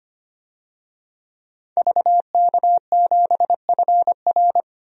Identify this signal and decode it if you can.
Morse code, heard as a keyed tone, about 25 wpm, 700 Hz. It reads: VK7FR
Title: DXCC-100 Call Signs - 5 Characters - 25wpm